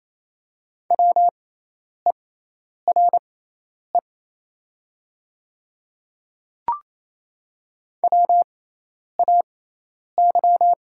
Morse code, heard as a keyed tone, about 27 wpm, spaced as for 10 wpm, 690 Hz. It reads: WERE WAY
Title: 6x Character Spacing - Top 100 Words - 28wpm